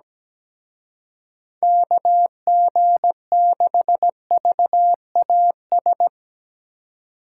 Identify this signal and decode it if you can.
Morse code, heard as a keyed tone, about 18 wpm, 700 Hz. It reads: KG6VAS
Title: Speed Racing - USA Call Signs 17wpm